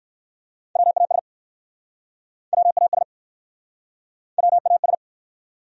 Morse code, heard as a keyed tone, about 61 wpm, 700 Hz. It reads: WAS WAS WAS